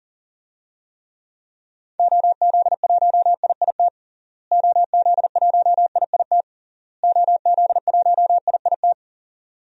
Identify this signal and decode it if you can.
Morse code, heard as a keyed tone, about 40 wpm, 700 Hz. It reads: OZ1IIT OZ1IIT OZ1IIT
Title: DXCC-100 Call Signs - 6 Characters - Repeated 3x - 40wpm